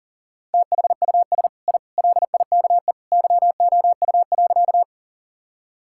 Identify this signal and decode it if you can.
Morse code, heard as a keyed tone, about 37 wpm, 700 Hz. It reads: THUS I LIKE YOU.
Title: Sentences from Top 1000 Words - Review Part 5 - 40wpm